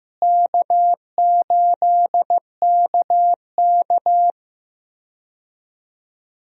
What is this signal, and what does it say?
Morse code, heard as a keyed tone, about 15 wpm, 700 Hz. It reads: K8KK
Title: USA Call Signs - 4 Characters - Repeated 2x - 15wpm